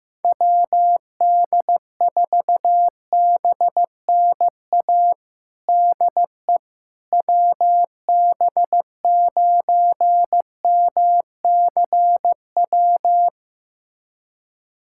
Morse code, heard as a keyed tone, about 15 wpm, 700 Hz. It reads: WD4BNA DE WB9MCW
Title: USA Callsign DE Callsign - 6 Characters - 15wpm